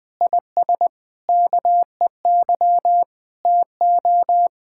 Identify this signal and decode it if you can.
Morse code, heard as a keyed tone, about 20 wpm, 700 Hz. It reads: IS KEY TO
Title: Difficult Sentences with Hints - 20wpm